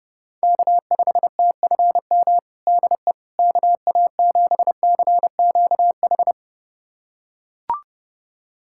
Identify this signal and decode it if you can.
Morse code, heard as a keyed tone, about 32 wpm, 700 Hz. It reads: K5TFM DE KA7CQH
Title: USA Callsign DE Callsign 30wpm